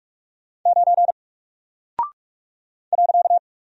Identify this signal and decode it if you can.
Morse code, heard as a keyed tone, about 46 wpm, 700 Hz. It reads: 9 .
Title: Single Character (Letter:Number:Punctuation:Prosign) Alt 45wpm